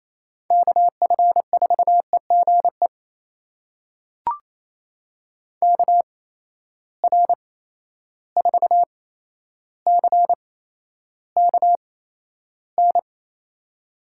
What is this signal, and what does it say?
Morse code, heard as a keyed tone, about 28 wpm, spaced as for 12 wpm, 700 Hz. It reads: KF4EGE KR4CKN
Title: USA Call Signs - 8x Character Spacing - 28wpm